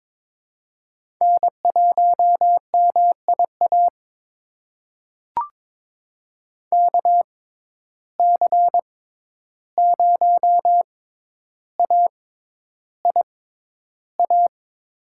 Morse code, heard as a keyed tone, about 22 wpm, spaced as for 11 wpm, 700 Hz. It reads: N1MIA KC0AIA
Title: USA Call Signs - 6x Character Spacing - 22wpm